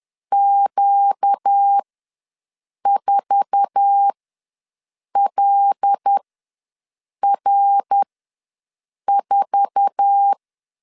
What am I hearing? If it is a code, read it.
Morse code, heard as a keyed tone, about 10 wpm, spaced as for 6 wpm, 800 Hz. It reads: Q4LR4
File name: LEZIONI / 8 YJ45 / 8 YJ4 ws 10wpm 4020